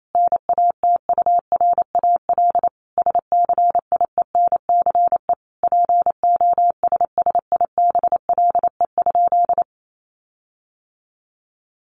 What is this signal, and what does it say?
Morse code, heard as a keyed tone, about 28 wpm, 700 Hz. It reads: NATURAL SCIENCE POSSIBLE?
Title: Sentences from Top 1500 Words 28wpm